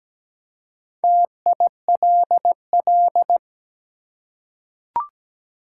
Morse code, heard as a keyed tone, about 17 wpm, 700 Hz. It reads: TILL
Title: Four Letter Words - 17wpm